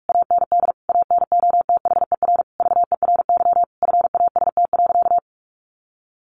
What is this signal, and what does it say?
Morse code, heard as a keyed tone, about 44 wpm, 700 Hz. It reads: AND ANOTHER VERY FAST.
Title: Sentences from Top 1100 Words 45wpm